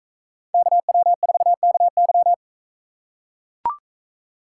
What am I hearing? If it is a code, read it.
Morse code, heard as a keyed tone, about 41 wpm, 700 Hz. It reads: KW4KY